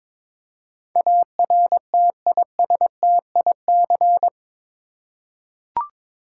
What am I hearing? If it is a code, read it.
Morse code, heard as a keyed tone, about 22 wpm, 700 Hz. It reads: ARTISTIC
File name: practice / 22wpm / Eight Letter Words - 22wpm